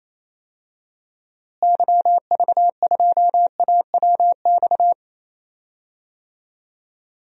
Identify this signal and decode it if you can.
Morse code, heard as a keyed tone, about 28 wpm, 700 Hz. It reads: YV2AWX
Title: DXCC-100 Call Signs - 6 Characters - Repeated 2x - 28wpm